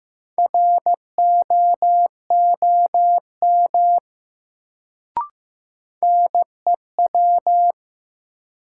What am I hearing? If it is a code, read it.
Morse code, heard as a keyed tone, about 15 wpm, 700 Hz. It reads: ROOM NEW
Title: Sentences from Top 300 Words - New Word Drill - 15wpm